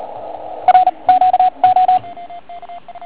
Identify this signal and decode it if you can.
Morse code, heard as a keyed tone, about 38 wpm, 740 Hz, with radio noise. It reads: R Q O
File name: Archiv / CUC138 / OK1IF CUC138